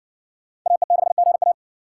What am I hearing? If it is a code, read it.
Morse code, heard as a keyed tone, about 58 wpm, 700 Hz. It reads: AE6GA